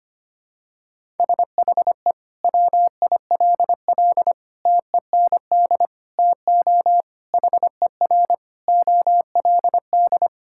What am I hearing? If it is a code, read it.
Morse code, heard as a keyed tone, about 25 wpm, 700 Hz. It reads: SHE WILL TEND TO HER OLD
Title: Sentences from Top 1000 Words - Review Part 5 - 25wpm